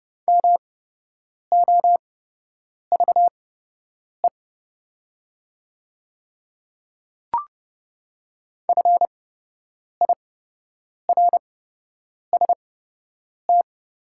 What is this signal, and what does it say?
Morse code, heard as a keyed tone, about 30 wpm, spaced as for 8 wpm, 700 Hz. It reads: MOVE FIRST
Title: 8x Character Spacing - Top 200 Words - 30wpm